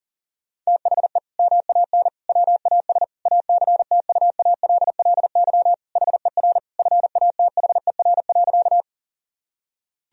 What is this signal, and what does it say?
Morse code, heard as a keyed tone, about 40 wpm, 700 Hz. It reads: THE MAN WAS ACTUALLY HER FATHER.
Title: Sentences from Top 1000 Words - Review Part 4 - 40wpm